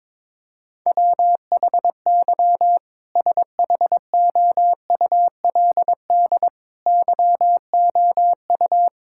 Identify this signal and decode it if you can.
Morse code, heard as a keyed tone, about 22 wpm, 700 Hz. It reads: WHY SHOULD YOU
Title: Sentences from Top 1000 Words - Review Part 6 - 22wpm